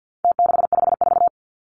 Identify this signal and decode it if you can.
Morse code, heard as a keyed tone, about 50 wpm, 700 Hz. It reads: T654
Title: Numbers - Contest Style - 4-Digit 50wpm